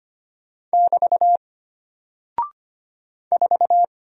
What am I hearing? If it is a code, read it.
Morse code, heard as a keyed tone, about 25 wpm, 700 Hz. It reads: <BT> 4
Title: Single Character (Letter:Number:Punctuation:Prosign) Alt 25wpm